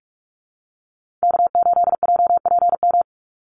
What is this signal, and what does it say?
Morse code, heard as a keyed tone, about 44 wpm, 700 Hz. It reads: K8JPM
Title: Call Signs Repeated 3x - 45wpm